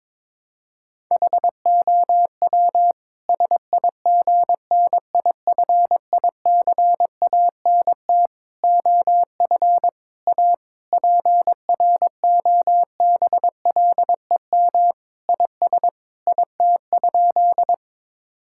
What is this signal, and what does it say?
Morse code, heard as a keyed tone, about 22 wpm, 700 Hz. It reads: HOW SIGNIFICANT OF A PROBLEM IS IT?
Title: Sentences from Top 1000 Words - Review Part 6 - 22wpm